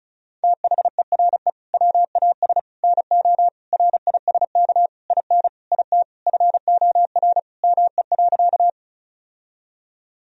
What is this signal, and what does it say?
Morse code, heard as a keyed tone, about 36 wpm, 700 Hz. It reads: THERE WAS NO RISK IN IT FOR ME.
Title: Sentences from Top 1000 Words - Review Part 4 - 35wpm